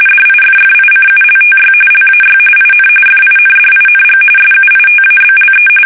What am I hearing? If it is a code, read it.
RTTY, about 45.45 baud. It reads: DIFFICULTY LAY IN FINDING SUITABLE
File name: fsk850 / benjamin-g-lamme